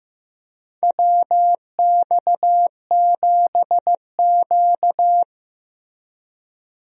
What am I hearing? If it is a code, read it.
Morse code, heard as a keyed tone, about 16 wpm, 700 Hz. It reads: WX7Q
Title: USA Call Signs - 4 Characters - Repeated 3x - 15wpm